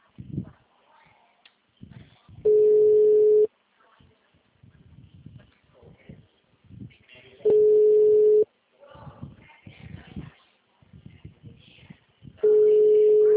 Ringback tone (Europe, 425 Hz).
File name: out- -2021